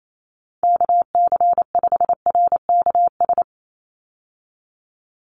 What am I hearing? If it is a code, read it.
Morse code, heard as a keyed tone, about 28 wpm, 700 Hz. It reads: KC5RKS
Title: USA Call Signs 28wpm